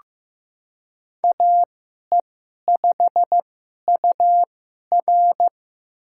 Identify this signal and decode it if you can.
Morse code, heard as a keyed tone, about 15 wpm, spaced as for 11 wpm, 700 Hz. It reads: AE5UR
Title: USA Call Signs - 5 Characters - 2x Character Spacing - 15wpm